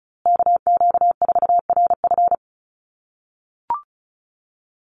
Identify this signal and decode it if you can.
Morse code, heard as a keyed tone, about 34 wpm, 700 Hz. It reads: KQ4RF
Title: USA Call Signs Repeated 3x 35wpm